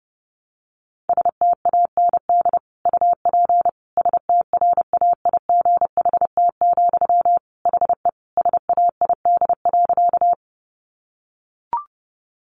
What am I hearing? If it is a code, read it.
Morse code, heard as a keyed tone, about 30 wpm, 700 Hz. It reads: STAND UP STRAIGHT, HE SAID.
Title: Sentences from Top 1000 Words 30wpm